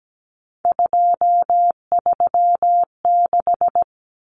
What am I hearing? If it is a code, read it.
Morse code, heard as a keyed tone, about 17 wpm, 700 Hz. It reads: 236
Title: Numbers - 3-Digit 17wpm